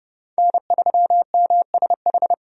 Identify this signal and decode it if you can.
Morse code, heard as a keyed tone, about 30 wpm, 700 Hz. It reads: N3MSH
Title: USA Call Signs - 5 Characters - 30wpm